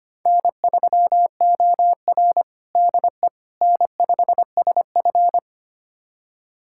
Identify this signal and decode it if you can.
Morse code, heard as a keyed tone, about 25 wpm, 700 Hz. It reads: N3OR DE N5SF
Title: USA Callsign DE Callsign - 4 Characters - 25wpm